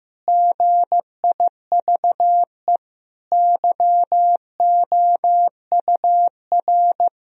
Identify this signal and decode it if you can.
Morse code, heard as a keyed tone, about 15 wpm, 700 Hz. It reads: GIVE YOUR
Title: Sentences from Top 1000 Words - Review Part 4 - 15wpm